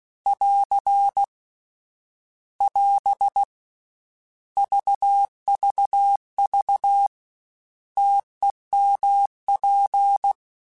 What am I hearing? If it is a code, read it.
Morse code, heard as a keyed tone, about 16 wpm, 800 Hz. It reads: <AR> <AS> VVV TEMP